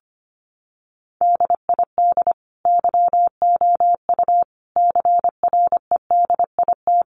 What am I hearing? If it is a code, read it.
Morse code, heard as a keyed tone, about 25 wpm, 700 Hz. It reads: DID YOU CREDIT
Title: Sentences from Top 1000 Words 25wpm